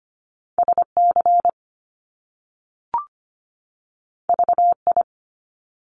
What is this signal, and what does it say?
Morse code, heard as a keyed tone, about 25 wpm, 700 Hz. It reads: SC VI
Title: Top Two Letter Combinations in Words 25wpm